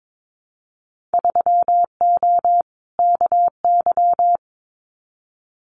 Morse code, heard as a keyed tone, about 22 wpm, 700 Hz. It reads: 3O KY
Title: Winter Field Day Exchanges 22wpm